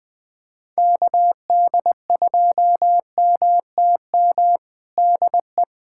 Morse code, heard as a keyed tone, about 21 wpm, 700 Hz. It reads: KD2MTM DE
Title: USA Callsign DE Callsign - 6 Characters - 20wpm